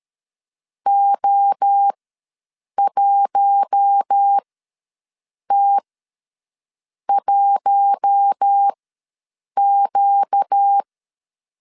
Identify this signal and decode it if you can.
Morse code, heard as a keyed tone, about 13 wpm, spaced as for 6 wpm, 800 Hz. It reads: O1T1Q
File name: LEZIONI / 7 GF9Z / 7 GF9Z ws 12wpm 3321